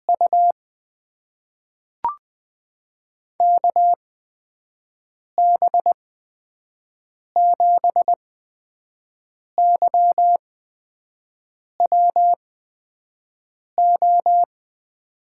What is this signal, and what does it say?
Morse code, heard as a keyed tone, about 20 wpm, spaced as for 5 wpm, 700 Hz. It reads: U KB7YWO
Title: USA Call Signs - 8x Character Spacing - 20wpm